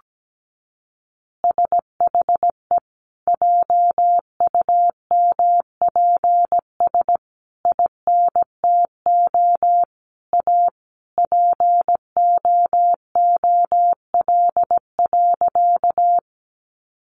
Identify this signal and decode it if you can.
Morse code, heard as a keyed tone, about 17 wpm, 700 Hz. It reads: SHE JUMPS INTO A POOL.
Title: Easy Phrases 17wpm